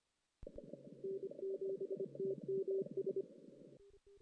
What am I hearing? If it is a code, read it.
Morse code, heard as a keyed tone, about 26 wpm, 410 Hz. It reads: N7TMS
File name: 2016 JAN CW NAQP